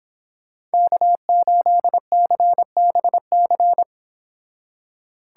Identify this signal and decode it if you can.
Morse code, heard as a keyed tone, about 26 wpm, 700 Hz. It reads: K8CBC